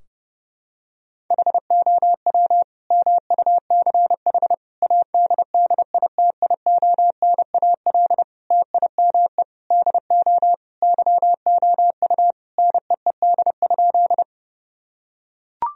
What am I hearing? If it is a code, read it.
Morse code, heard as a keyed tone, about 30 wpm, 700 Hz. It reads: HOW MUCH ADDITIONAL TIME DO YOU NEED?
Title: Sentences from Top 1000 Words - Review Part 4 - 30wpm